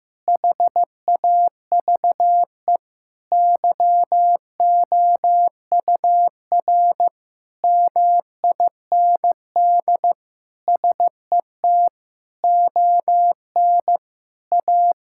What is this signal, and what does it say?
Morse code, heard as a keyed tone, about 15 wpm, 700 Hz. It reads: HAVE YOUR MIND SET ON A
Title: Sentences from Top 1000 Words - Review Part 5 - 15wpm